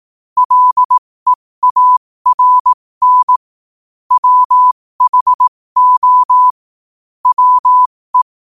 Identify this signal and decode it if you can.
Morse code, heard as a keyed tone, about 18 wpm, spaced as for 15 wpm, 1000 Hz. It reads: LEARN WHO WE
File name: qotdfiles / 15wpm